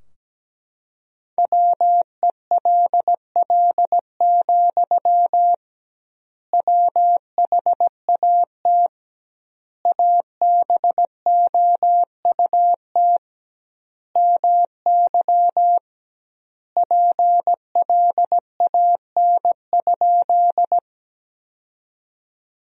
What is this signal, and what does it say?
Morse code, heard as a keyed tone, about 17 wpm, 700 Hz. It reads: WELL, WHAT ABOUT MY PLAN?
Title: 2x Word-Spacing - Sentences from Top 500 Words - 17wpm